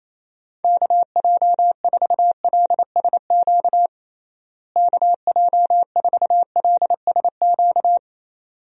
Morse code, heard as a keyed tone, about 28 wpm, 700 Hz. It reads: KJ4LSQ KJ4LSQ
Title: USA Call Signs - 6 Characters - Repeated 2x - 28wpm